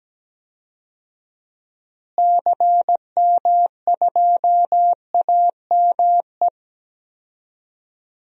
Morse code, heard as a keyed tone, about 17 wpm, 700 Hz. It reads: CM2AME
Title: DXCC-100 Call Signs - 6 Characters - Repeated 2x - 17wpm